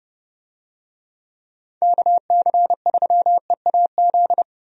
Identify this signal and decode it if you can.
Morse code, heard as a keyed tone, about 31 wpm, 700 Hz. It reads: KC3EAZ
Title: USA Call Signs - 6 Characters - 30wpm